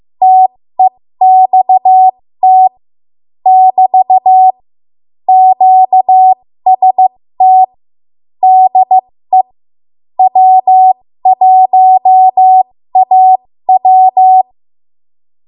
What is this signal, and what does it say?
Morse code, heard as a keyed tone, about 15 wpm, spaced as for 13 wpm, 750 Hz. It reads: TEXT <BT> QST DE W1AW
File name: Morse / 13WPM